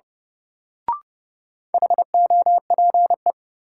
Morse code, heard as a keyed tone, about 30 wpm, 700 Hz. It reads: HOPE
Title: Sentences from Top 1000 Words - New Word Drill - 30wpm